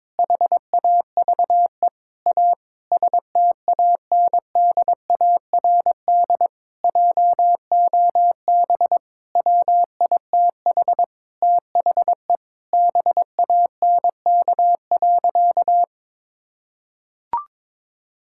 Morse code, heard as a keyed tone, about 21 wpm, 700 Hz. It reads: HAVE A STANDARD JOB WITH THE BANK.
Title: Sentences from Top 1000 Words - Review Part 6 - 22wpm